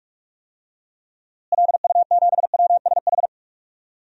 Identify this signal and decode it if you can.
Morse code, heard as a keyed tone, about 46 wpm, 700 Hz. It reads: LU7WSH
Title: DXCC-100 Call Signs - 6 Characters - Repeated 3x - 45wpm